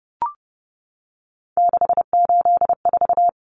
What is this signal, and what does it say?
Morse code, heard as a keyed tone, about 30 wpm, 700 Hz. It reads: 684
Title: Numbers - 3-Digit 30wpm